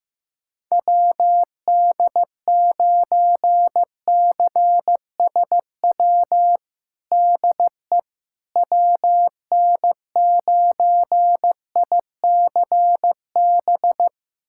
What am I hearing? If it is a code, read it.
Morse code, heard as a keyed tone, about 15 wpm, 700 Hz. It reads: WD9CSW DE WN9ICB
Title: USA Callsign DE Callsign 15wpm